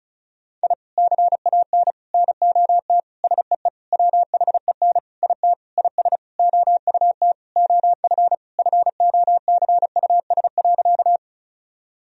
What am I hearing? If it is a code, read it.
Morse code, heard as a keyed tone, about 34 wpm, 700 Hz. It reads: I CAN NOT SEE WHEN IT IS OUT OF FOCUS.
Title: Sentences from Top 1000 Words - Review Part 4 - 35wpm